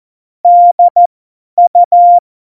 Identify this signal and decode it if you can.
Morse code, heard as a keyed tone, about 14 wpm, spaced as for 10 wpm, 700 Hz. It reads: DU
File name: Morsekurs / Witze / Witz 31.Juli 2018, 50 F70